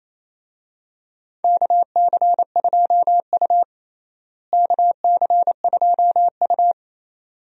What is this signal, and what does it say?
Morse code, heard as a keyed tone, about 28 wpm, 700 Hz. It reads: KC2U KC2U
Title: USA Call Signs - 4 Characters - Repeated 2x - 28wpm